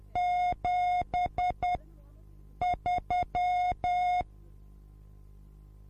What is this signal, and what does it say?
Morse code, heard as a keyed tone, about 10 wpm, spaced as for 6 wpm, 720 Hz. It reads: 73